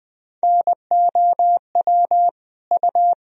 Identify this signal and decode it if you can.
Morse code, heard as a keyed tone, about 20 wpm, 700 Hz. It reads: NOW U
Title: Sentences from Top 100 Words and Abbreviations 20wpm